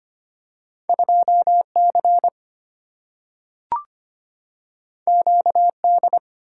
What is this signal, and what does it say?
Morse code, heard as a keyed tone, about 25 wpm, 700 Hz. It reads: 2 C Q D
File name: Two Letter-Number Combinations - 25wpm